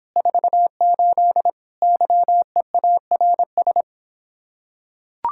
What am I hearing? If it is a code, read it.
Morse code, heard as a keyed tone, about 26 wpm, 700 Hz. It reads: 48 YEARS